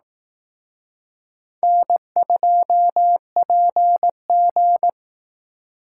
Morse code, heard as a keyed tone, about 18 wpm, 700 Hz. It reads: N2PG
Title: Speed Racing - USA Call Signs 15wpm